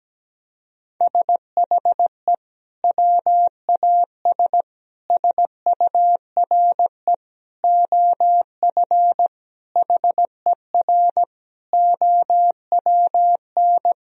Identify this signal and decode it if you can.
Morse code, heard as a keyed tone, about 17 wpm, 700 Hz. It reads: SHE WAS SURE OF HER OWN
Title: Sentences from Top 1000 Words - Review Part 1 - 17wpm